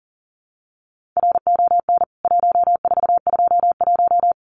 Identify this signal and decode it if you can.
Morse code, heard as a keyed tone, about 40 wpm, 700 Hz. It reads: RON 1421
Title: CWT Exchange 40wpm